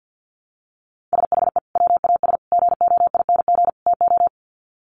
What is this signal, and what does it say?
Morse code, heard as a keyed tone, about 50 wpm, 700 Hz. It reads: SHE WAS GOING TO
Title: Sets of 4 Words 50wpm